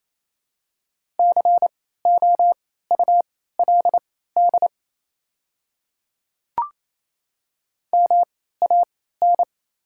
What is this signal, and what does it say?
Morse code, heard as a keyed tone, about 28 wpm, spaced as for 16 wpm, 700 Hz. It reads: COULD MAN
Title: 3x Character Spacing - Top 200 Words - 28wpm